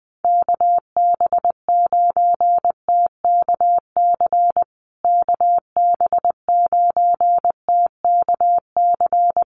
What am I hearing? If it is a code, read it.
Morse code, heard as a keyed tone, about 20 wpm, 700 Hz. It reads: KB9TKC KB9TKC
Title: USA Call Signs Repeated 2x 20wpm